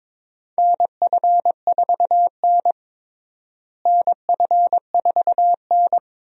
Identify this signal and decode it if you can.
Morse code, heard as a keyed tone, about 22 wpm, 700 Hz. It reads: NF4N NF4N
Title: USA Call Signs - 4 Characters - Repeated 2x - 22wpm